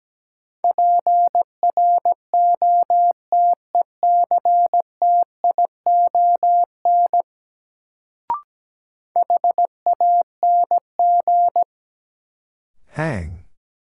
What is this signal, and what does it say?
Morse code, heard as a keyed tone, about 17 wpm, 700 Hz. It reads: PROTECTION HANG
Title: Sentences from Top 1600 Words - New Word Drill Alt - 17wpm